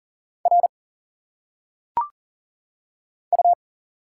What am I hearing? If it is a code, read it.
Morse code, heard as a keyed tone, about 40 wpm, 700 Hz. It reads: R U
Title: Single Letters - Phonetic Pronunciation 40wpm